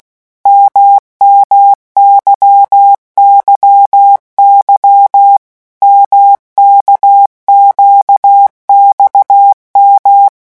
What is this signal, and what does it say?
Morse code, heard as a keyed tone, about 16 wpm, 800 Hz. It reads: MMYYY MKQXM
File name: morse mp3 / revision1-vitesse 15 mots